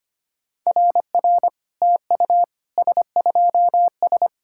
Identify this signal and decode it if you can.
Morse code, heard as a keyed tone, about 25 wpm, 690 Hz. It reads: RR TU S2S
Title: SOTA QSO Practice 25wpm